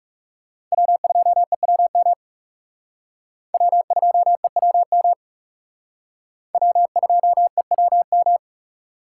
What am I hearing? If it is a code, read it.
Morse code, heard as a keyed tone, about 36 wpm, 700 Hz. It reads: W2EWM W2EWM W2EWM
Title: Speed Racing - USA 1x3 Call Signs 30wpm